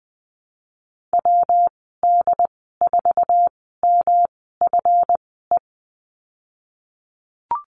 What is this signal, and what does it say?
Morse code, heard as a keyed tone, about 20 wpm, spaced as for 15 wpm, 700 Hz. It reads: WD4MFE
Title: 173 Lesson 28g - Number 9 - Call Signs 20wpm